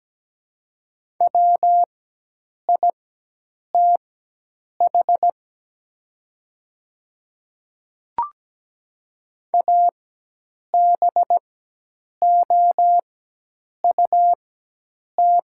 Morse code, heard as a keyed tone, about 17 wpm, spaced as for 8 wpm, 700 Hz. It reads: WITH ABOUT
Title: 4x Character Spacing - Top 100 Words - 17wpm